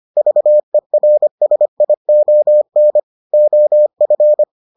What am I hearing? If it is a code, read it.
Morse code, heard as a keyed tone, about 25 wpm, 600 Hz. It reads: VERSION OF